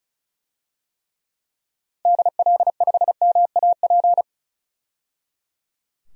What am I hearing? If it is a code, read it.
Morse code, heard as a keyed tone, about 36 wpm, 700 Hz. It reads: DL5MAP
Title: DXCC-100 Call Signs - 6 Characters - 35wpm